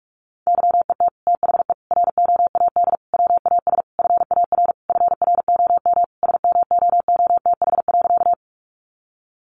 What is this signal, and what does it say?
Morse code, heard as a keyed tone, about 44 wpm, 700 Hz. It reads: YET THE ROAD WAS FAR FROM SMOOTH.
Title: Sentences from Top 1400 Words 45wpm